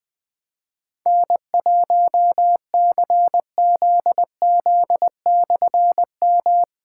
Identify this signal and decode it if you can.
Morse code, heard as a keyed tone, about 20 wpm, 700 Hz. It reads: N1CZZ/M
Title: Difficult Call Signs 20wpm